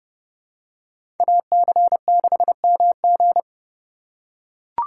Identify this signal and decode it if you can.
Morse code, heard as a keyed tone, about 30 wpm, 700 Hz. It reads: AC6MG
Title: USA Call Signs - 5 Characters - 3x Character Spacing - 30wpm